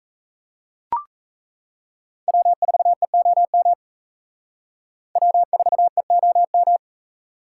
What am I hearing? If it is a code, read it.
Morse code, heard as a keyed tone, about 35 wpm, 700 Hz. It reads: W4EOM W4EOM
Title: Speed Racing - USA 1x3 Call Signs 28wpm